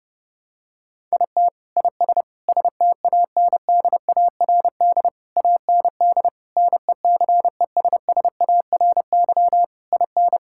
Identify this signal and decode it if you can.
Morse code, heard as a keyed tone, about 30 wpm, 700 Hz. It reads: IT IS STANDARD AND NECESSARY IN
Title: Sentences from Top 1000 Words - Review Part 6 - 30wpm